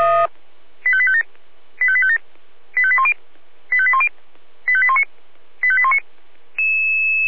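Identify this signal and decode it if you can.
Five-tone selective call (ZVEI), heard as quick runs of five tones, tones of about 70 ms. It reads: 85857 85857 85810 85810 85819 85819
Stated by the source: ZVEI five-tone selective call received on an SDR.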